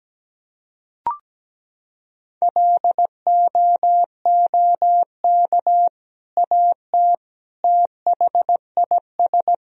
Morse code, heard as a keyed tone, about 17 wpm, 700 Hz. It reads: LOOK AT THIS
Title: Sentences from Top 1000 Words - Review Part 2 - 17wpm